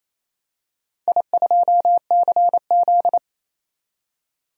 Morse code, heard as a keyed tone, about 28 wpm, 700 Hz. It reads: I2CZ
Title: DXCC-100 Call Signs - 4 Characters - 28wpm